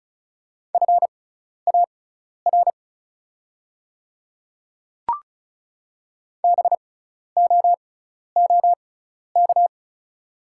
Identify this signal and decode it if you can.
Morse code, heard as a keyed tone, about 36 wpm, spaced as for 12 wpm, 700 Hz. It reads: FAR BOOK
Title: 6x Character Spacing - Top 300 Words - 35wpm